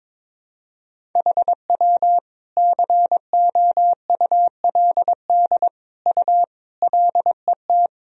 Morse code, heard as a keyed tone, about 22 wpm, 700 Hz. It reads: HW COULD U LET
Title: Sentences from Top 300 Words and Abbreviations 22wpm